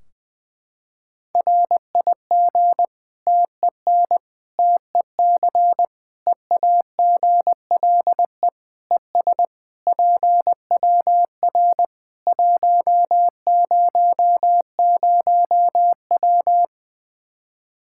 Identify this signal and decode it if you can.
Morse code, heard as a keyed tone, about 20 wpm, 700 Hz. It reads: RIG TEN TEC EAGLE ES PWR 100W
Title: Practice QSO - Exchange 2 - Info - 20wpm